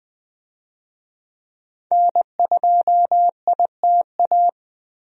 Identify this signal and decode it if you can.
Morse code, heard as a keyed tone, about 20 wpm, 700 Hz. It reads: N2ITA
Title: USA Call Signs - 5 Characters - Repeated 3x - 20wpm